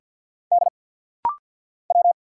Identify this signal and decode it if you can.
Morse code, heard as a keyed tone, about 50 wpm, 690 Hz. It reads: D W
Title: Single Letters - Rapid-Fire 50wpm